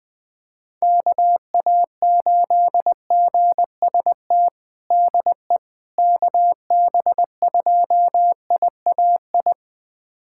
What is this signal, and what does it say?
Morse code, heard as a keyed tone, about 21 wpm, 700 Hz. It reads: KA8GST DE KB2IAI
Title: USA Callsign DE Callsign - 6 Characters - 20wpm